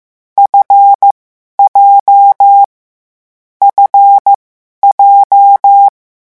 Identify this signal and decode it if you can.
Morse code, heard as a keyed tone, about 15 wpm, spaced as for 11 wpm, 800 Hz. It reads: FJ FJ
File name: morse mp3 / lecon7-vitesse 14